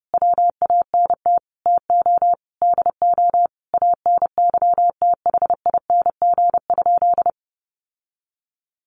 Morse code, heard as a keyed tone, about 30 wpm, 700 Hz. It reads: WANT TO DO ANYTHING?
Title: Sentences from Top 1000 Words 30wpm